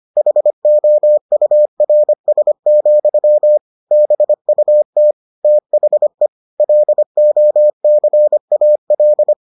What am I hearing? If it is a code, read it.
Morse code, heard as a keyed tone, about 25 wpm, 600 Hz. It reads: HOURS, BUT THE LOCAL